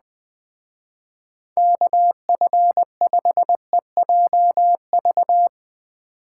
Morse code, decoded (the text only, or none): KF5EJV